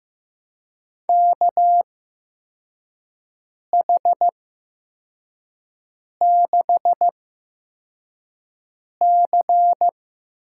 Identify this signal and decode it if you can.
Morse code, heard as a keyed tone, about 15 wpm, spaced as for 4 wpm, 700 Hz. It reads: KH6C
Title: USA Call Signs - 6 Characters - 8x Character Spacing - 15wpm